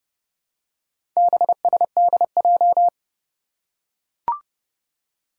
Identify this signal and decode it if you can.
Morse code, heard as a keyed tone, about 29 wpm, 700 Hz. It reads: BSDJ
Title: Four Challenging Letter-Number Combinations 30wpm